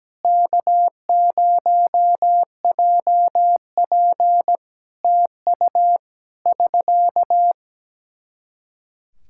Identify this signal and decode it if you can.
Morse code, heard as a keyed tone, about 17 wpm, 700 Hz. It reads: K0JP TU <SK>
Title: Practice QSO - Exchange 4 - Ending - 17wpm